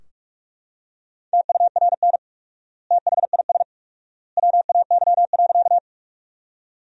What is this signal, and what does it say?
Morse code, heard as a keyed tone, about 44 wpm, 700 Hz. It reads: TURN THIS WAY.
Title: 4x Word-Spacing - Sentences from Top 200 Words - 45wpm